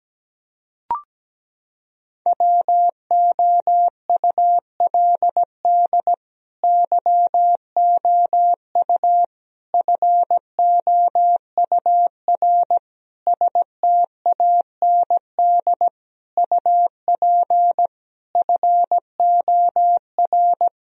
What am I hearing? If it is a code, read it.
Morse code, heard as a keyed tone, about 17 wpm, 700 Hz. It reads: WOULD YOU FOUR STAND UP FOR
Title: Sentences from Top 1000 Words - Review Part 1 - 17wpm